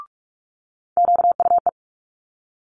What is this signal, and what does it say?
Morse code, heard as a keyed tone, about 46 wpm, 700 Hz. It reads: QUE
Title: Multi-letter Phonemes Quick Fire 45wpm